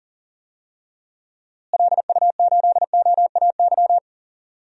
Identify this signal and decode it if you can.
Morse code, heard as a keyed tone, about 38 wpm, 700 Hz. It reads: LU8OAY
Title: DXCC-100 Call Signs - 6 Characters - Repeated 3x - 40wpm